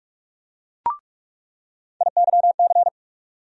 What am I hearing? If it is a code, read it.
Morse code, heard as a keyed tone, about 46 wpm, 700 Hz. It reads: IYC
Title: USA Call Sign Postfixes - 3 Letters 45wpm